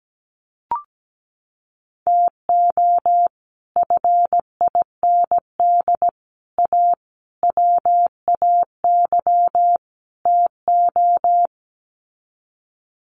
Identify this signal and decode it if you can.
Morse code, heard as a keyed tone, about 17 wpm, 700 Hz. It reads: TO FIND A WAY TO
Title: Sets of 5 Words 17wpm